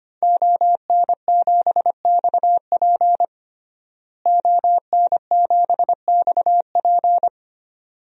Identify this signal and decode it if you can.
Morse code, heard as a keyed tone, about 25 wpm, 700 Hz. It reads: ON7XP ON7XP
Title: DXCC-100 Call Signs - 5 Characters - Repeated 2x - 25wpm